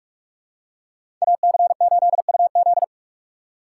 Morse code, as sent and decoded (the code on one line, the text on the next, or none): .- -.-. ---.. ..- --..
AC8UZ